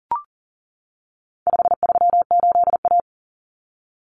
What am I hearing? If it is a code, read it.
Morse code, heard as a keyed tone, about 37 wpm, 700 Hz. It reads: 538A
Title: Common Cut Numbers - 4-Digit 40wpm